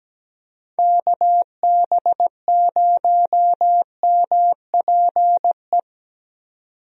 Morse code, heard as a keyed tone, about 17 wpm, 700 Hz. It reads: KB0MPE
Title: USA Call Signs - 6 Characters - 17wpm